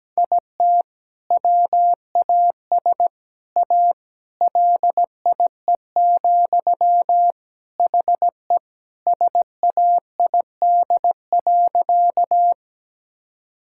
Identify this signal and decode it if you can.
Morse code, heard as a keyed tone, about 17 wpm, 700 Hz. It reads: IT WAS A LIE, HE SAID.
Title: Sentences from Top 1600 Words 17wpm